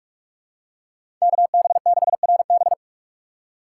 Morse code, heard as a keyed tone, about 47 wpm, 700 Hz. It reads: KB6RB
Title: USA Call Signs - 6x Character Spacing - 45wpm